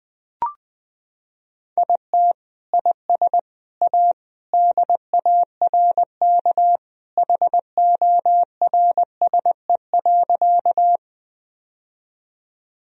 Morse code, heard as a keyed tone, about 20 wpm, 700 Hz. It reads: IT IS A DARK HORSE.
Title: Sentences from Top 1000 Words - Review Part 1 - 20wpm